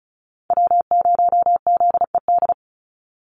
Morse code, heard as a keyed tone, about 34 wpm, 700 Hz. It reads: W0ZED
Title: USA Call Signs 35wpm